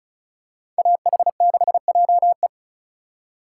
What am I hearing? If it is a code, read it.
Morse code, heard as a keyed tone, about 36 wpm, 700 Hz. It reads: AH6JE
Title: USA Call Signs - 5 Characters - 2x Character Spacing - 35wpm